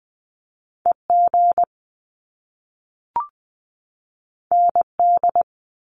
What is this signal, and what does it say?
Morse code, heard as a keyed tone, about 20 wpm, 700 Hz. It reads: EG ND
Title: Top Two Letter Combinations in Words 20wpm